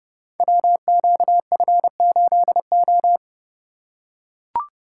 Morse code, heard as a keyed tone, about 30 wpm, 700 Hz. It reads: WQF8O
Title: Code Groups - 5 Letter:Numbers - 30wpm